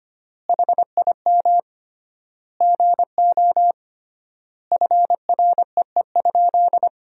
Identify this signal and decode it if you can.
Morse code, heard as a keyed tone, about 25 wpm, 700 Hz. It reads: HIM GO FREE?
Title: 3x Word-Spacing - Sentences from Top 500 Words - 25wpm